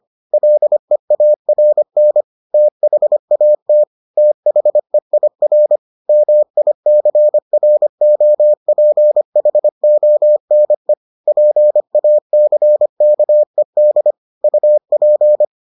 Morse code, heard as a keyed tone, about 25 wpm, 600 Hz. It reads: LEARN THAT THEIR MICROPHONE PACKED UP